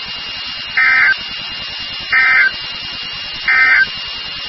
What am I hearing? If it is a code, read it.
Emergency Alert System (SAME) tones, heard as harsh buzzing bursts. It is heard from 0.7 to 3.9 s.
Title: NOAA Weather Radio WXK36